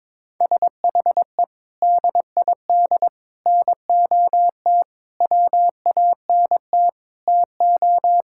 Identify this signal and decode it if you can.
Morse code, heard as a keyed tone, about 22 wpm, 700 Hz. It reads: SHE DID NOT WANT TO
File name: practice / 22wpm / Sentences with 4-Word Phrases - 22wpm